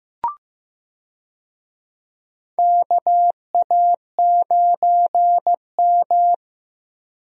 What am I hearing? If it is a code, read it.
Morse code, heard as a keyed tone, about 15 wpm, 700 Hz. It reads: KA9M
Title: USA Call Signs - 4 Characters - 15wpm